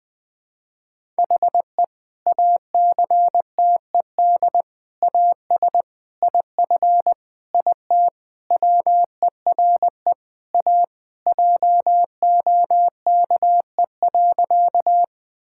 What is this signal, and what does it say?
Morse code, heard as a keyed tone, about 21 wpm, 700 Hz. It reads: HE ACTED AS IF IT WERE A JOKE.
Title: Sentences with 5-Word Phrases - Spoken Beginning - 20wpm